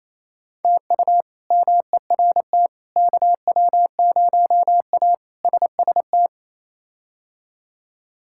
Morse code, heard as a keyed tone, about 28 wpm, 700 Hz. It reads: TU MERT KW0A SST
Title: K1USN SST QSO Practice - 28wpm